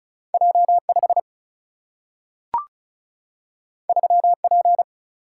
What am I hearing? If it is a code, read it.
Morse code, heard as a keyed tone, about 36 wpm, 700 Hz. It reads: J5 3P
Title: Single-Letter Single-Number 35wpm